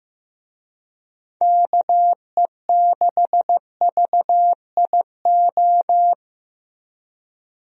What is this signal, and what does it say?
Morse code, heard as a keyed tone, about 15 wpm, 700 Hz. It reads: KE6VIO
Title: USA Call Signs - 6 Characters - 15wpm